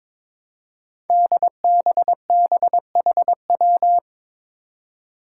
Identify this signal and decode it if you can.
Morse code, heard as a keyed tone, about 22 wpm, 700 Hz. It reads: DBBHW
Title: Five Challenging Letter-Number Combinations 22wpm